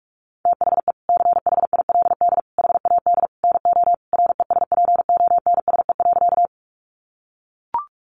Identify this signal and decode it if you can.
Morse code, heard as a keyed tone, about 44 wpm, 700 Hz. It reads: THE CHILD HAD NO RESPONSE.
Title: Sentences from Top 1400 Words 45wpm